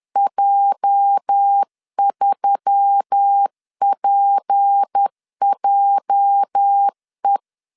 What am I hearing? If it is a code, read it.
Morse code, heard as a keyed tone, about 10 wpm, 800 Hz. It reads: J3PJE
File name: LEZIONI / 8 YJ45 / 8 YJ 10wpm 2651